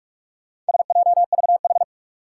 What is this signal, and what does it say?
Morse code, heard as a keyed tone, about 46 wpm, 700 Hz. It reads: SJVH
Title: Four Challenging Letter-Number Combinations 45wpm